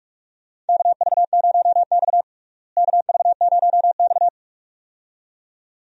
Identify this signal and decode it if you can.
Morse code, heard as a keyed tone, about 46 wpm, 700 Hz. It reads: KV0X KV0X
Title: USA Call Signs - 4 Characters - Repeated 2x - 45wpm